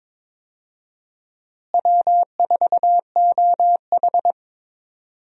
Morse code, heard as a keyed tone, about 21 wpm, 700 Hz. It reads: W4OH
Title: USA Call Signs - 4 Characters - Repeated 3x - 22wpm